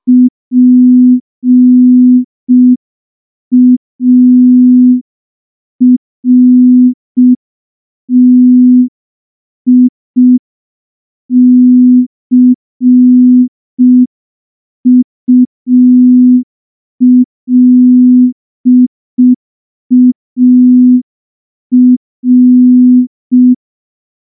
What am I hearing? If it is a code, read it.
Morse code, heard as a keyed tone, about 5 wpm, 250 Hz. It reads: PARTICULAR
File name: morse / morse-training-05WPM